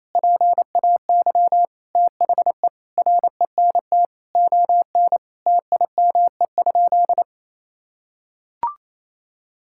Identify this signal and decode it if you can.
Morse code, heard as a keyed tone, about 28 wpm, 700 Hz. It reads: PAY THE RENT ON TIME?
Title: Sentences from Top 1000 Words - Review Part 5 - 28wpm